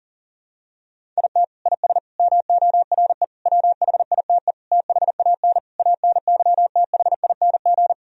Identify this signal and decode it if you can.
Morse code, heard as a keyed tone, about 40 wpm, 700 Hz. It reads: IT IS MORE WHITE THAN ANYTHING
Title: Sentences from Top 1000 Words - Review Part 1 - 40wpm